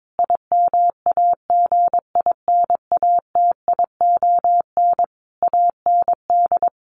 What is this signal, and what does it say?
Morse code, heard as a keyed tone, about 22 wpm, 700 Hz. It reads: IMAGINATION AND
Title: Sentences from Top 1300 Words 22wpm